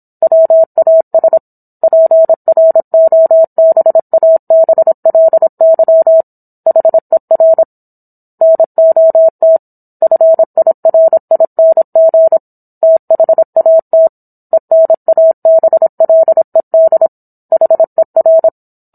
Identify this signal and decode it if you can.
Morse code, heard as a keyed tone, about 27 wpm, 650 Hz. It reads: WAS PROBABLY HER NOT FIRING THAT ENABLED HER
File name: Book One / 17-Chapter 17 - 26 WPM